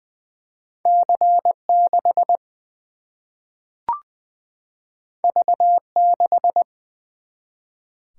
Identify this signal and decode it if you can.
Morse code, heard as a keyed tone, about 20 wpm, 700 Hz. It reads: C 6 V 6
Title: Single-Letter Single-Number 20wpm